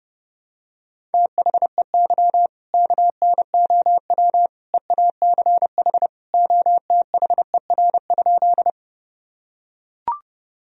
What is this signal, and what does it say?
Morse code, heard as a keyed tone, about 30 wpm, 700 Hz. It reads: THEY KNOW EACH OTHER?
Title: Sentences from Top 1000 Words - Review Part 1 - 30wpm